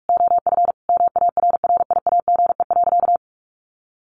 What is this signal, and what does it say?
Morse code, heard as a keyed tone, about 44 wpm, 700 Hz. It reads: OF MARRIAGE.
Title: Sentences from Top 800 Words 45wpm